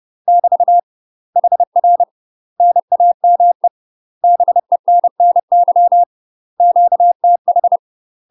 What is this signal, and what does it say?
Morse code, heard as a keyed tone, about 30 wpm, 700 Hz. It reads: <BT> HR NAME BENNY QTH
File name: UK MASTER QSOs TRAINING / 30WPM / UK MASTER QSOs 0017